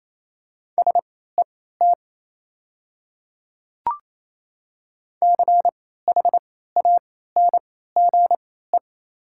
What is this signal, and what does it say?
Morse code, heard as a keyed tone, about 27 wpm, spaced as for 16 wpm, 690 Hz. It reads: SET CHANGE
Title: 3x Character Spacing - Top 200 Words - 28wpm